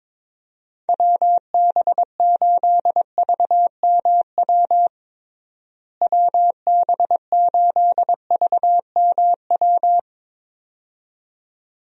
Morse code, heard as a keyed tone, about 22 wpm, 700 Hz. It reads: WB8VMW WB8VMW
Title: USA Call Signs - 6 Characters - Repeated 2x - 22wpm